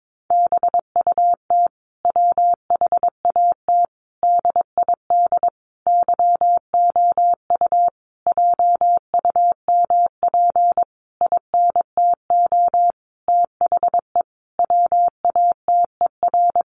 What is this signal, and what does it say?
Morse code, heard as a keyed tone, about 22 wpm, 700 Hz. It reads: BUT WHAT DID YOU JUMP INTO THE WATER
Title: Sentences from Top 1300 Words 22wpm